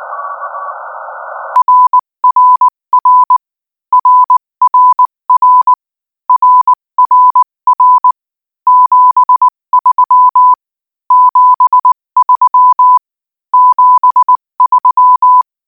Morse code, heard as a keyed tone, about 19 wpm, 1000 Hz. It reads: RRR RRR RRR 73 73 73
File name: IZ4BEH-13-cm-AS